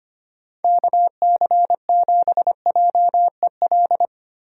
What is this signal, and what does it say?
Morse code, heard as a keyed tone, about 25 wpm, 700 Hz. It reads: KC7JEL
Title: USA Call Signs - 6 Characters - 6x Character Spacing - 25wpm